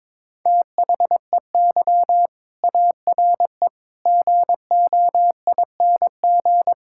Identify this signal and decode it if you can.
Morse code, heard as a keyed tone, about 22 wpm, 700 Hz. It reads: THEY ARE GOING
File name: practice / 22wpm / Sentences with 4-Word Phrases - 22wpm